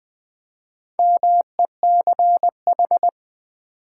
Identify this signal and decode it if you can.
Morse code, heard as a keyed tone, about 20 wpm, 700 Hz. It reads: MECH
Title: Build A Word - 4 Letters At A Time - 8-Letter Words - 20wpm